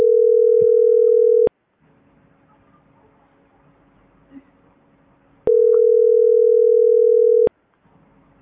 Ringback tone (North America), 440 + 480 Hz.